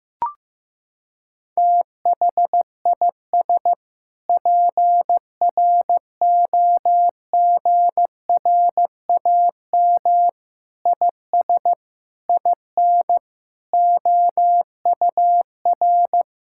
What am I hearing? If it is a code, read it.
Morse code, heard as a keyed tone, about 15 wpm, 700 Hz. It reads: THIS PROGRAM IS IN OUR
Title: Sentences from Top 1000 Words - Review Part 5 - 15wpm